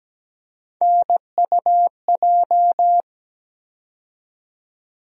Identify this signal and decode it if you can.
Morse code, heard as a keyed tone, about 17 wpm, 700 Hz. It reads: NUJ
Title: USA Call Sign Postfixes - 3 Letters 17wpm